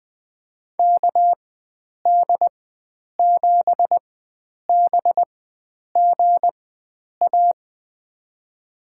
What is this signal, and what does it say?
Morse code, heard as a keyed tone, about 20 wpm, spaced as for 9 wpm, 700 Hz. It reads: KD7BGA
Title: USA Call Signs - 6 Characters - 4x Character Spacing - 20wpm